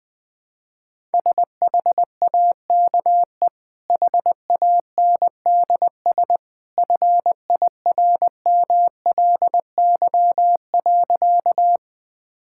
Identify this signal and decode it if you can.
Morse code, heard as a keyed tone, about 21 wpm, 700 Hz. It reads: SHAKE HANDS FIRMLY.
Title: Easy Sentences with Hints - 20wpm